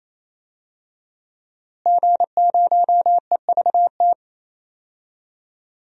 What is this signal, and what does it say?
Morse code, heard as a keyed tone, about 28 wpm, 700 Hz. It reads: G0EVT
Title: DXCC-100 Call Signs - 5 Characters - Repeated 3x - 28wpm